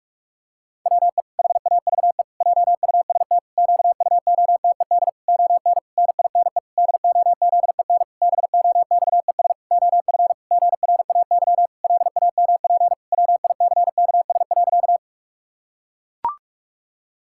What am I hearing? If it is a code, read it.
Morse code, heard as a keyed tone, about 41 wpm, 700 Hz. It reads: WE HAVE JUST QUOTED ON NINE DOZEN BOXES OF GRAY LAMP WICKS.
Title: Pangram Challenge v2 - 45wpm